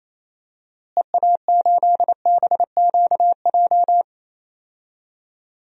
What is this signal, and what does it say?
Morse code, heard as a keyed tone, about 28 wpm, 700 Hz. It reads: EA8BQJ
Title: DXCC-100 Call Signs - 6 Characters - Repeated 2x - 28wpm